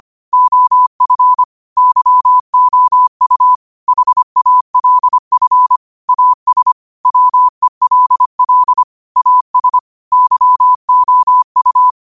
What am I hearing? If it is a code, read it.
Morse code, heard as a keyed tone, about 25 wpm, 1000 Hz. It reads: OF YOU HALF AS WELL AS YOU
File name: qotdfiles / 25wpm